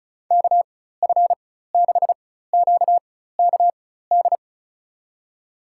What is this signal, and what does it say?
Morse code, heard as a keyed tone, about 36 wpm, spaced as for 16 wpm, 700 Hz. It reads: KF6QKD
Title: USA Call Signs - 6 Characters - 4x Character Spacing - 35wpm